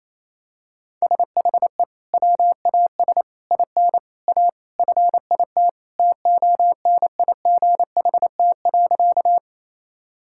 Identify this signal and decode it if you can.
Morse code, heard as a keyed tone, about 28 wpm, 700 Hz. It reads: SHE WAS IN A FIT TONIGHT.
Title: Sentences from Top 1000 Words - Review Part 5 - 28wpm